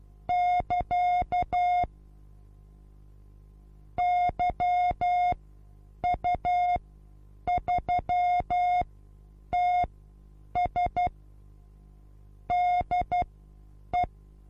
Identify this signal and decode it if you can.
Morse code, heard as a keyed tone, about 12 wpm, spaced as for 8 wpm, 720 Hz. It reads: <KA> YU3TS DE
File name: repos / Morsekurs / DJ4UF Morsen23